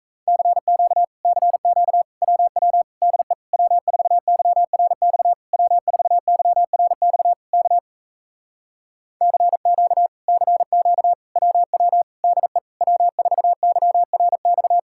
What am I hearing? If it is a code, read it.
Morse code, heard as a keyed tone, about 38 wpm, 700 Hz. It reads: CQ CQ WW DE W4YRX W4YRX K CQ CQ WW DE W4YRX